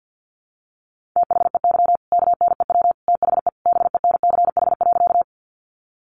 Morse code, heard as a keyed tone, about 48 wpm, 700 Hz. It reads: THEY KNEW THE BENCH.
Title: Sentences from Top 1400 Words 50wpm